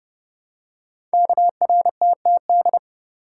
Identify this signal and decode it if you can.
Morse code, heard as a keyed tone, about 30 wpm, 700 Hz. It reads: KRTTD
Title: Code Groups - 5 Letter:Numbers - 30wpm